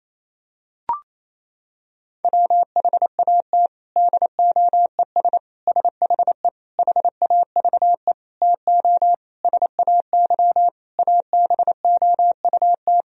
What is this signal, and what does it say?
Morse code, heard as a keyed tone, about 28 wpm, 700 Hz. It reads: WHAT DOES SHE HAVE TO SAY ABOUT
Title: Sentences from Top 1000 Words - Review Part 3 - 28wpm